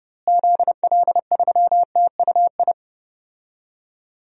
Morse code, heard as a keyed tone, about 31 wpm, 700 Hz. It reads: ZL3TUI
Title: DXCC-100 Call Signs - 6 Characters - Repeated 2x - 30wpm